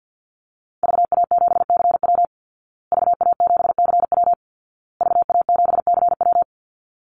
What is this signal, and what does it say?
Morse code, heard as a keyed tone, about 49 wpm, 700 Hz. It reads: VA7CW VA7CW VA7CW
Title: Call Signs Repeated 2x - 50wpm